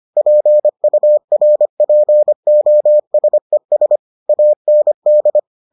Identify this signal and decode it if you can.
Morse code, heard as a keyed tone, about 25 wpm, 600 Hz. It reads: PURPOSES AND